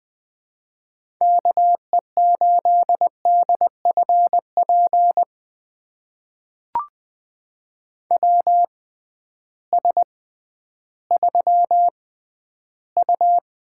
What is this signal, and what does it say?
Morse code, heard as a keyed tone, about 20 wpm, spaced as for 12 wpm, 700 Hz. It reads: KE8DFP WS3U
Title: USA Call Signs - 6x Character Spacing - 20wpm